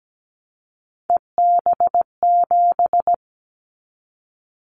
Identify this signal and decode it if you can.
Morse code, heard as a keyed tone, about 17 wpm, 700 Hz. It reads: EB7
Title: Callsign Prefixes for Top 25 DXCC - 17wpm